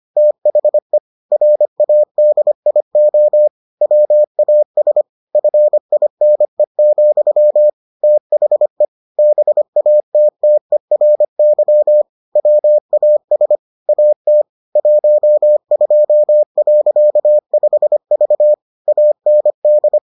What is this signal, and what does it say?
Morse code, heard as a keyed tone, about 25 wpm, 600 Hz. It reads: THE RADIO WAS FINE, THE BATTERY WAS AT 12.5V AND